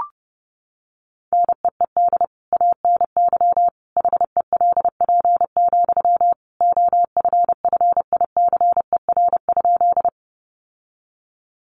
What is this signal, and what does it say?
Morse code, heard as a keyed tone, about 30 wpm, 700 Hz. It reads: NEED ANY HELP, OFFICER?
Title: Sentences from Top 1200 Words 30wpm